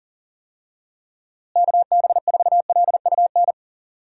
Morse code, heard as a keyed tone, about 40 wpm, 700 Hz. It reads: KB4LUN
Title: USA Call Signs - 6 Characters - Repeated 2x - 40wpm